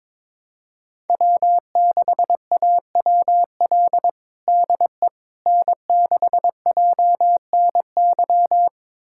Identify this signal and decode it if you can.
Morse code, heard as a keyed tone, about 22 wpm, 700 Hz. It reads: W6AWL DE N6JNY
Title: USA Callsign DE Callsign - 5 Characters - 22wpm